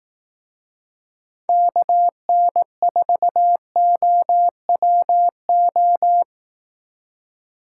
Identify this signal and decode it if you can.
Morse code, heard as a keyed tone, about 17 wpm, 700 Hz. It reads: KN4OWO
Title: Speed Racing - USA 2x3 Call Signs 15wpm